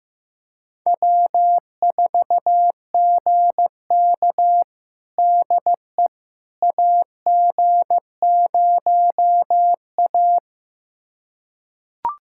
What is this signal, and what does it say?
Morse code, heard as a keyed tone, about 16 wpm, 700 Hz. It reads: W4GK DE AG0A
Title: USA Callsign DE Callsign - 4 Characters - 15wpm